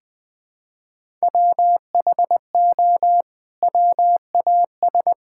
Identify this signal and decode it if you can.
Morse code, heard as a keyed tone, about 20 wpm, 700 Hz. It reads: WHO WAS
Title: Sentences from Top 1000 Words - Review Part 5 - 20wpm